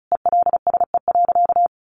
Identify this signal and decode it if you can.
Morse code, heard as a keyed tone, about 34 wpm, 700 Hz. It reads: ELSE.
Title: Sentences from Top 800 Words 35wpm